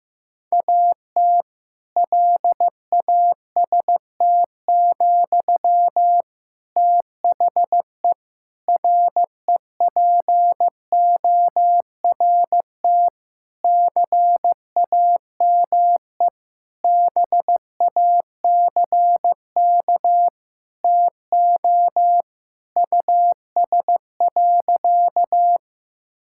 Morse code, heard as a keyed tone, about 16 wpm, 700 Hz. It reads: AT LAST, THE REPORT CAME BACK TO US.
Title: Sentences from Top 1000 Words - Review Part 6 - 15wpm